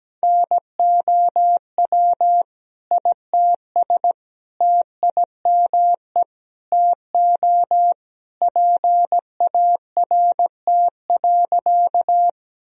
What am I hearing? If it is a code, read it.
Morse code, heard as a keyed tone, about 17 wpm, 700 Hz. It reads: NOW ITS TIME TO PART.
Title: Sentences from Top 1000 Words - Review Part 1 - 17wpm